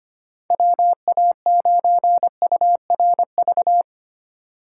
Morse code, heard as a keyed tone, about 33 wpm, 700 Hz. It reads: WA9URV